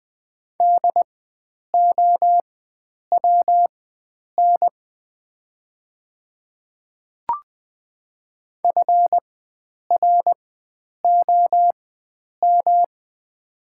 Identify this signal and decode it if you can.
Morse code, heard as a keyed tone, about 21 wpm, spaced as for 9 wpm, 700 Hz. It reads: DOWN FROM
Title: 4x Character Spacing - Top 100 Words - 20wpm